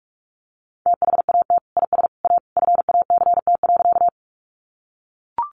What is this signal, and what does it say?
Morse code, heard as a keyed tone, about 44 wpm, 700 Hz. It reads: THAT IS A FACT.
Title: Sentences from Top 500 Words 45wpm